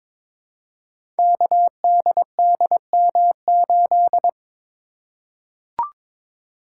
Morse code, heard as a keyed tone, about 22 wpm, 700 Hz. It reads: KDDM8
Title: Code Groups - 5 Letter:Numbers - 22wpm